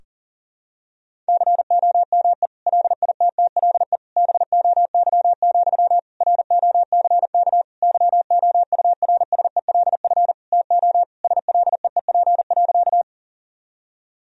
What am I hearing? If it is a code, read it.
Morse code, heard as a keyed tone, about 40 wpm, 700 Hz. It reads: COME LITTLE BOY, ROCK YOURSELF TO SLEEP.
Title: Sentences from Top 1000 Words - Review Part 2 - 40wpm